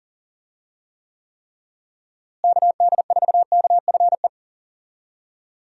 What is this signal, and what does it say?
Morse code, heard as a keyed tone, about 40 wpm, 700 Hz. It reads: KD4KFE
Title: USA Call Signs - 6 Characters - Repeated 3x - 40wpm